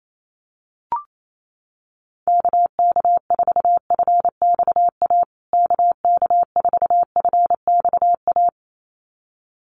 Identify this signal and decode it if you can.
Morse code, heard as a keyed tone, about 28 wpm, 700 Hz. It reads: KK4FXA KK4FXA
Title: USA Call Signs Repeated 2x 28wpm